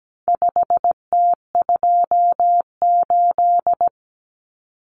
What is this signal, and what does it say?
Morse code, heard as a keyed tone, about 17 wpm, 700 Hz. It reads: 5T28
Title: Common Cut Numbers - 4-Digit 17wpm